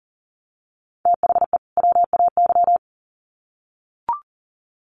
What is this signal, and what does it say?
Morse code, heard as a keyed tone, about 40 wpm, 700 Hz. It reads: THE WAY
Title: Sets of 2 Words 40wpm